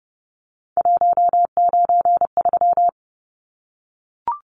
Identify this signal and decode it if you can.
Morse code, heard as a keyed tone, about 30 wpm, 700 Hz. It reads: 193
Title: Numbers - 3-Digit 30wpm